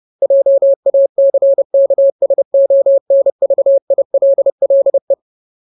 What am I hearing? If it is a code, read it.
Morse code, heard as a keyed tone, about 30 wpm, 550 Hz. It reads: JACKSONVILLE
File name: Ultimate Word List / 30WPM Ultimate